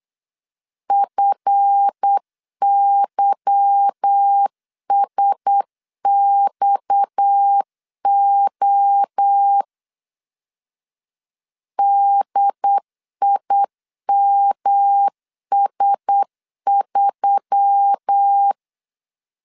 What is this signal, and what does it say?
Morse code, heard as a keyed tone, about 8 wpm, 800 Hz. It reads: FYSXO DIMS3
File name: LEZIONI / 8 YJ45 / 8 YJ4 8wpm 3246